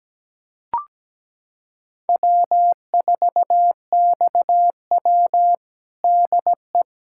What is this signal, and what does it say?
Morse code, heard as a keyed tone, about 17 wpm, 700 Hz. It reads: W4XW DE
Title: USA Callsign DE Callsign - 4 Characters - 17wpm